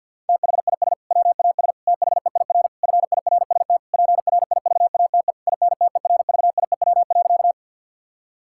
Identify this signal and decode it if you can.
Morse code, heard as a keyed tone, about 50 wpm, 700 Hz. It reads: THIS WAS THEIR FIRST PRIVATE INTERVIEW.
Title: Sentences from Top 1600 Words 50wpm